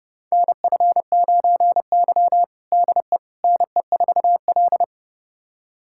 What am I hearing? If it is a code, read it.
Morse code, heard as a keyed tone, about 30 wpm, 700 Hz. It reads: NF9Y DE NE4L
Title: USA Callsign DE Callsign - 4 Characters - 30wpm